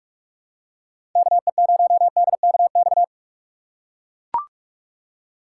Morse code, heard as a keyed tone, about 44 wpm, 700 Hz. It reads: KE0DKX
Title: USA Call Signs - 6 Characters - 2x Character Spacing - 45wpm